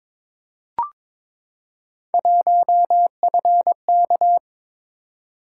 Morse code, heard as a keyed tone, about 22 wpm, 700 Hz. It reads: 1FK
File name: practice / 22wpm / Three Letter-Number Combinations - 22wpm